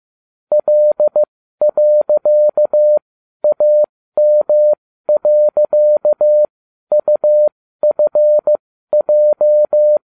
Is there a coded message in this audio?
Yes: Morse code from 0.5 to 10.0 s, heard as a keyed tone, about 15 wpm, spaced as for 12 wpm, 600 Hz. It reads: L.AM.UFJ